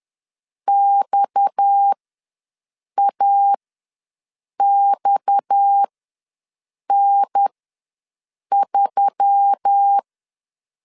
Morse code, heard as a keyed tone, about 10 wpm, spaced as for 6 wpm, 800 Hz. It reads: XAXN3
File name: LEZIONI / 6 PKX0 / 6 PKX ws 10wpm 3706